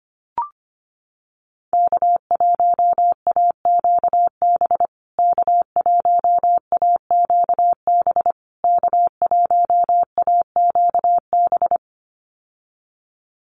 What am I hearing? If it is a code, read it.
Morse code, heard as a keyed tone, about 25 wpm, 700 Hz. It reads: K1AQB K1AQB K1AQB
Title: USA Call Signs Repeated 3x 25wpm